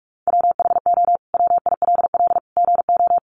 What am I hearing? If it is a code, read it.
Morse code, heard as a keyed tone, about 44 wpm, 700 Hz. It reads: WHO WILL GO
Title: Sentences from Top 100 Words 45wpm